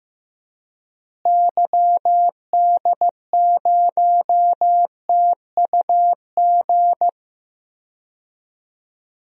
Morse code, heard as a keyed tone, about 15 wpm, 700 Hz. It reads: YD0TUG
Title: DXCC-100 Call Signs - 6 Characters - Repeated 2x - 15wpm